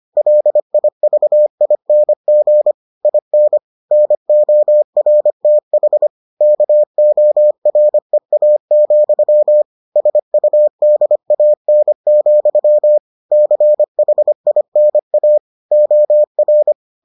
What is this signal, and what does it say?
Morse code, heard as a keyed tone, about 25 wpm, 600 Hz. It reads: LIVING IN NORTH KOREA, SUDAN, CHINA OR